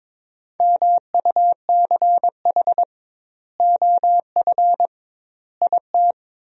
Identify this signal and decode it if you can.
Morse code, heard as a keyed tone, about 22 wpm, 700 Hz. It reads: MUCH OF IT
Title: 2x Word-Spacing - Sentences from Top 400 Words - 22wpm